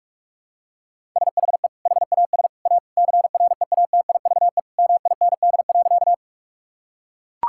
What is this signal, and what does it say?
Morse code, heard as a keyed tone, about 42 wpm, 700 Hz. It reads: SHE HAS A CREATIVE MIND.
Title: Sentences from Top 1000 Words - Review Part 5 - 45wpm